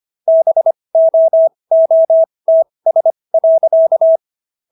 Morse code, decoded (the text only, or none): BOOTS.